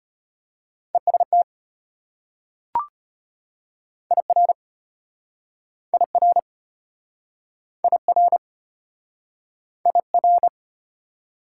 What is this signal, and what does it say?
Morse code, heard as a keyed tone, about 36 wpm, 690 Hz. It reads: EST IR IR IR IR